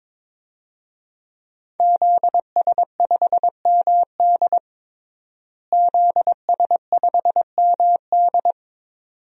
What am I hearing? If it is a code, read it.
Morse code, heard as a keyed tone, about 22 wpm, 700 Hz. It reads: ZS5MD ZS5MD
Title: DXCC-100 Call Signs - 5 Characters - Repeated 2x - 22wpm